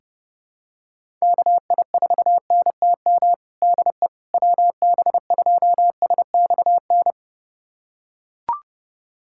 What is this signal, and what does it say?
Morse code, heard as a keyed tone, about 30 wpm, 700 Hz. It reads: KI4NTM DE WB2SXN
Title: USA Callsign DE Callsign 30wpm